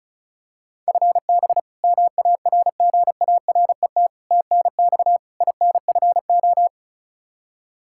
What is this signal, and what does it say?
Morse code, heard as a keyed tone, about 36 wpm, 700 Hz. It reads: FB MARGARET TNX INFO
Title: Practice QSO - Exchange 3 - More Info - 35wpm